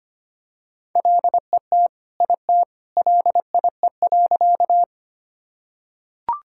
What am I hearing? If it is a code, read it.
Morse code, heard as a keyed tone, about 25 wpm, 700 Hz. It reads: LET IT LIE.
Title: Sentences from Top 1600 Words 25wpm